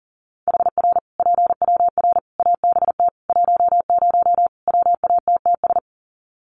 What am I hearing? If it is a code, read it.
Morse code, heard as a keyed tone, about 37 wpm, 700 Hz. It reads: HR PWR ABT 10 WATTS
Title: Common QSO Phrases 40wpm